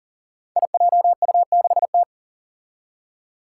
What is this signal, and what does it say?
Morse code, heard as a keyed tone, about 40 wpm, 700 Hz. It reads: IJU6T
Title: Five Challenging Letter-Number Combinations 40wpm